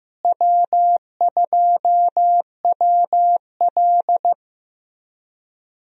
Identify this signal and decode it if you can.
Morse code, heard as a keyed tone, about 16 wpm, 700 Hz. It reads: W2WL
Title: USA Call Signs - 4 Characters - Repeated 3x - 15wpm